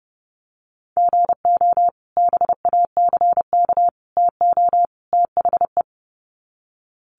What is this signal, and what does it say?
Morse code, heard as a keyed tone, about 30 wpm, 700 Hz. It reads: GO BACK TO THE
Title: Sets of 4 Words 30wpm